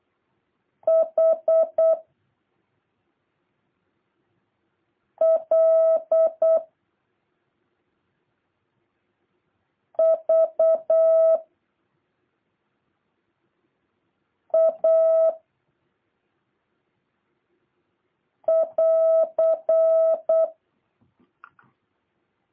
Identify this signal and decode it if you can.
Morse code, heard as a keyed tone, about 8 wpm, 650 Hz. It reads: H L V A <AR>
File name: jh8cbh / 2023cw